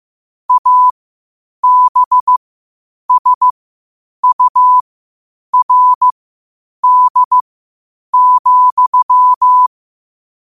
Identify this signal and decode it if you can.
Morse code, heard as a keyed tone, about 15 wpm, spaced as for 8 wpm, 1000 Hz. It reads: ABSURD,
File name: qotdfiles / 8wpm